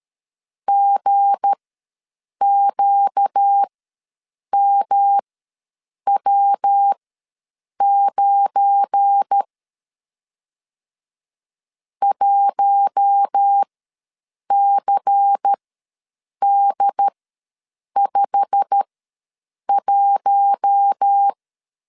Morse code, heard as a keyed tone, about 13 wpm, spaced as for 7 wpm, 800 Hz. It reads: GQMW9 1CD51
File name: LEZIONI / 8 YJ45 / 8 YJ45 ws 12wpm 3146